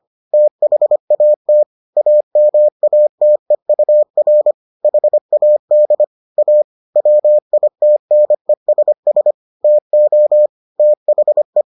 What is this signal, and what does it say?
Morse code, heard as a keyed tone, about 25 wpm, 600 Hz. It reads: THAT AMATEUR HAD A WITNESS TO THE